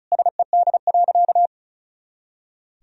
Morse code, heard as a keyed tone, about 34 wpm, 700 Hz. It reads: SED.
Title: Sentences from Top 100 Words and Abbreviations 35wpm